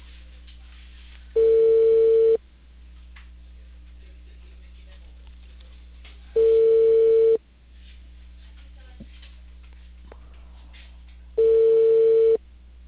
Ringback tone (China, 450 Hz).